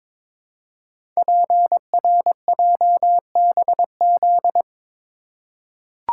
Morse code, heard as a keyed tone, about 22 wpm, 700 Hz. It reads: PRJBZ
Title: Code Groups - 5 Letters - 22wpm